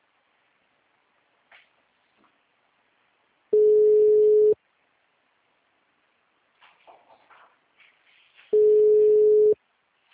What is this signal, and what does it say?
Ringback tone (Europe, 425 Hz).